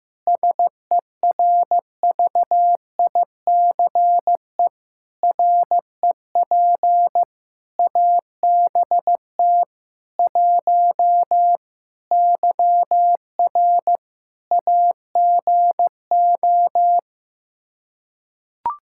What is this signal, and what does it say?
Morse code, heard as a keyed tone, about 15 wpm, 700 Hz. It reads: SERVICE REP ABT 1 YR AGO
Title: Practice QSO - Exchange 3 - More Info - 15wpm